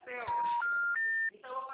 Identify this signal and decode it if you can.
Special information tone, with speech.